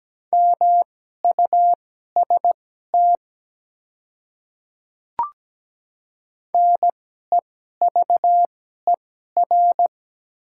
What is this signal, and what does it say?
Morse code, heard as a keyed tone, about 17 wpm, spaced as for 12 wpm, 700 Hz. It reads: MUST NEVER
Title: 2x Character Spacing - Top 300 Words - 17wpm